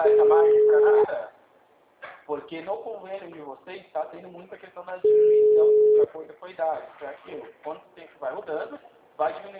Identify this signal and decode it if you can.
Ringback tone (Europe, 425 Hz), with speech.
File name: out- -2013- - - .398